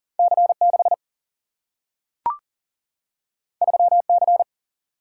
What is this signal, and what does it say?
Morse code, heard as a keyed tone, about 40 wpm, 700 Hz. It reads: C6 3C
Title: Single-Letter Single-Number 40wpm